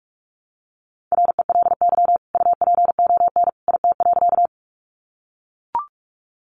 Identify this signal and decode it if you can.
Morse code, heard as a keyed tone, about 44 wpm, 700 Hz. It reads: RELY UPON IT.
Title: Sentences from Top 1200 Words 45wpm